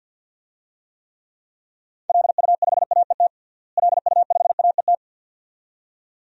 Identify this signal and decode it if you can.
Morse code, heard as a keyed tone, about 45 wpm, 700 Hz. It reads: LU5AET LU5AET
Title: DXCC-100 Call Signs - 6 Characters - Repeated 2x - 50wpm